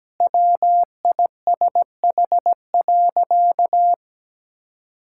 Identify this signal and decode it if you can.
Morse code, heard as a keyed tone, about 17 wpm, 700 Hz. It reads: WISH.
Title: Sentences from Top 1600 Words 17wpm